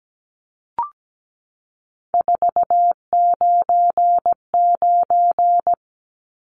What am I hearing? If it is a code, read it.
Morse code, heard as a keyed tone, about 17 wpm, 700 Hz. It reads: 499
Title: Numbers - 3-Digit 17wpm